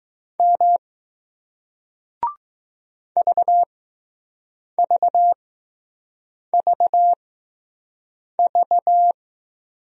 Morse code, heard as a keyed tone, about 20 wpm, spaced as for 6 wpm, 700 Hz. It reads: M VVVV